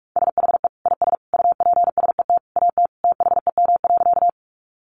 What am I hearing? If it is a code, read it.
Morse code, heard as a keyed tone, about 44 wpm, 700 Hz. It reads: SHE IS UPSET AT THEM.
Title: Sentences from Top 800 Words 45wpm